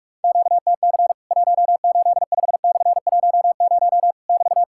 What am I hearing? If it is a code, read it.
Morse code, heard as a keyed tone, about 45 wpm, 700 Hz. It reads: QTC 185/10 <BT>
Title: Worked All Europe DX Contest - QTC Exchange - 45wpm